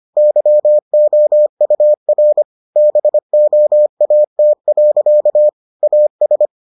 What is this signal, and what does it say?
Morse code, heard as a keyed tone, about 25 wpm, 600 Hz. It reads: YOUR BOAT. AS